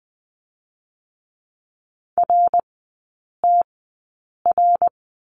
Morse code, heard as a keyed tone, about 20 wpm, spaced as for 8 wpm, 700 Hz. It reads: RTR
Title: 087 Lesson 15e - Number 5 - Three Characters 20wpm